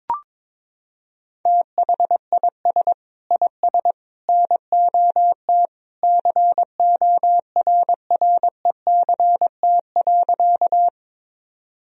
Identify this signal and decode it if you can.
Morse code, heard as a keyed tone, about 22 wpm, 700 Hz. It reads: THIS IS NOT CORRECT.
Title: Sentences from Top 1000 Words - Review Part 2 - 22wpm